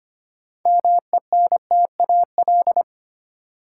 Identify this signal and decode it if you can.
Morse code, heard as a keyed tone, about 25 wpm, 700 Hz. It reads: MENTAL
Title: Sentences from Top 1000 Words - Review Part 4 - New Word Drill - 25wpm